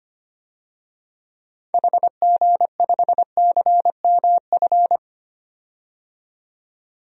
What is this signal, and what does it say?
Morse code, heard as a keyed tone, about 25 wpm, 700 Hz. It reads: HG5CMF
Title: DXCC-100 Call Signs - 6 Characters - 25wpm